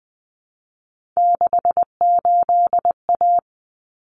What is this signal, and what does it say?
Morse code, heard as a keyed tone, about 20 wpm, 700 Hz. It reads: 68A
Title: Common Cut Numbers - 3-Digit 20wpm